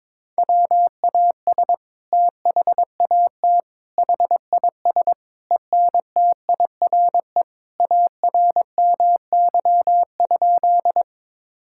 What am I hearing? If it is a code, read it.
Morse code, heard as a keyed tone, about 22 wpm, 700 Hz. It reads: WAS THAT HIS ENTIRE ARMY?
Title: Sentences from Top 1000 Words - Review Part 5 - 22wpm